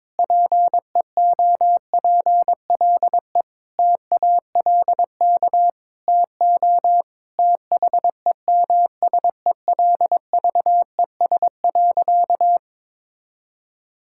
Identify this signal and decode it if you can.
Morse code, heard as a keyed tone, about 22 wpm, 700 Hz. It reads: PEOPLE TALK TO THEMSELVES.
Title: Sentences from Top 1000 Words - Review Part 5 - 22wpm